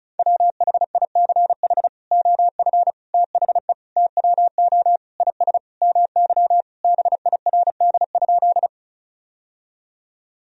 Morse code, heard as a keyed tone, about 36 wpm, 700 Hz. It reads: WHICH OF THE TWO IS MY BIRD?
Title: Sentences from Top 1000 Words - Review Part 2 - 35wpm